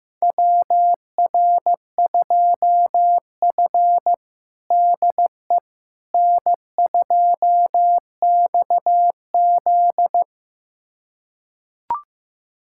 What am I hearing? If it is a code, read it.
Morse code, heard as a keyed tone, about 15 wpm, 700 Hz. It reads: WR2F DE N2XZ
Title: USA Callsign DE Callsign - 4 Characters - 15wpm